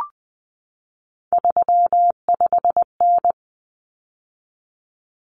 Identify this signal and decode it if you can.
Morse code, heard as a keyed tone, about 20 wpm, 700 Hz. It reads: 35N
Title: Common Cut Numbers - 3-Digit 20wpm